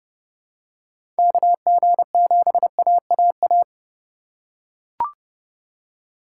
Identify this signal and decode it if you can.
Morse code, heard as a keyed tone, about 30 wpm, 700 Hz. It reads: KG7AAA
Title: USA Call Signs - 6 Characters - 2x Character Spacing - 30wpm